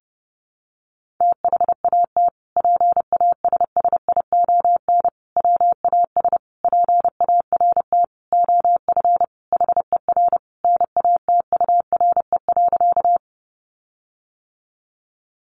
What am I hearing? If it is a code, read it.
Morse code, heard as a keyed tone, about 30 wpm, 700 Hz. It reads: THAT PASSION WAS PART OF HER NATURE.
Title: Sentences from Top 1300 Words 30wpm